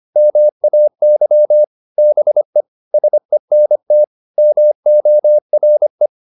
Morse code, heard as a keyed tone, about 25 wpm, 600 Hz. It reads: MAY BE SENT MORE